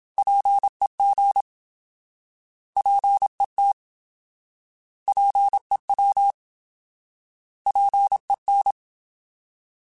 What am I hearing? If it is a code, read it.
Morse code, heard as a keyed tone, about 27 wpm, 800 Hz. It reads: PEG PET PEW PEN
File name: Alphabetical / Alphabetical mp3s / Ps @25